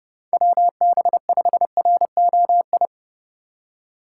Morse code, heard as a keyed tone, about 29 wpm, 700 Hz. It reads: WB5ROI